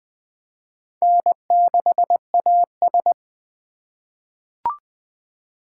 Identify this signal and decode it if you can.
Morse code, heard as a keyed tone, about 20 wpm, 700 Hz. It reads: N6AS
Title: USA Call Signs - 4 Characters - 4x Character Spacing - 20wpm